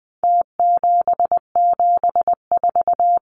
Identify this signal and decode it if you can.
Morse code, heard as a keyed tone, about 20 wpm, 700 Hz. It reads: T774
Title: Numbers - Contest Style - 4-Digit 20wpm